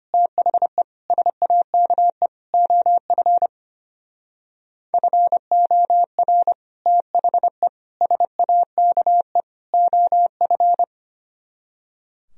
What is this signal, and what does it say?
Morse code, heard as a keyed tone, about 32 wpm, 700 Hz. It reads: THE SAKE OF FOR THE SAKE OF